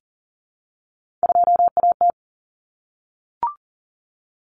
Morse code, heard as a keyed tone, about 40 wpm, 700 Hz. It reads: 2AT
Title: Common Cut Numbers - 3-Digit 40wpm